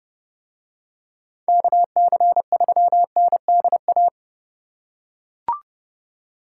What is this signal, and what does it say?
Morse code, heard as a keyed tone, about 27 wpm, 700 Hz. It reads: KC3NDA